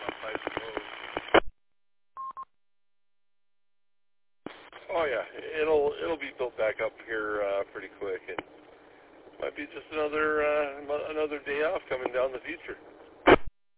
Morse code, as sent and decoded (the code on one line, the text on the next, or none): none